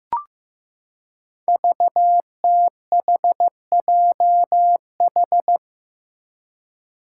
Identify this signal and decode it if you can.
Morse code, heard as a keyed tone, about 15 wpm, 700 Hz. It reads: VTHJH
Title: Five Challenging Letter-Number Combinations 15wpm